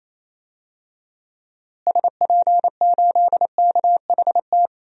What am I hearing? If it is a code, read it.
Morse code, heard as a keyed tone, about 27 wpm, 700 Hz. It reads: SP8KHT
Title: DXCC-100 Call Signs - 6 Characters - Repeated 2x - 28wpm